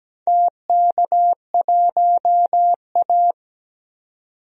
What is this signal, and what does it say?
Morse code, heard as a keyed tone, about 17 wpm, 700 Hz. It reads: TK1A
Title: DXCC-100 Call Signs - 4 Characters - 17wpm